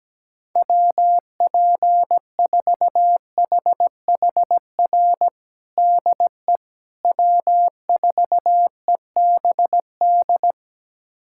Morse code, heard as a keyed tone, about 17 wpm, 700 Hz. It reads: WP4HHR DE W4EBD
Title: USA Callsign DE Callsign 17wpm